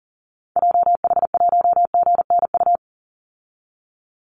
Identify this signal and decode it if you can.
Morse code, heard as a keyed tone, about 40 wpm, 700 Hz. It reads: JH1GNU
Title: Call Signs Repeated 2x - 40wpm